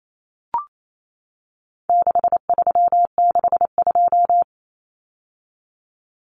Morse code, heard as a keyed tone, about 28 wpm, 700 Hz. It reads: 6362
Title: Numbers - 4-Digit 28wpm